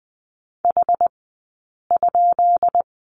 Morse code, heard as a keyed tone, about 20 wpm, 700 Hz. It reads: H ?
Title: 132 Lesson 22d - Question Mark - Two Characters 20wpm